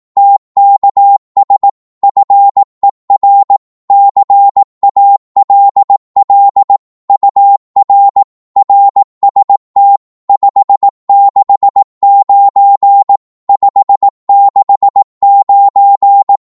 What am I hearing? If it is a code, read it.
Morse code, heard as a keyed tone, about 18 wpm, 800 Hz. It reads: TKS FER CALL UR RST 569 569
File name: Morse / qso-18wpm